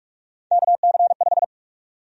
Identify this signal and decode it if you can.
Morse code, heard as a keyed tone, about 45 wpm, 700 Hz. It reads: KC5
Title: USA Call Sign Prefixes - 45wpm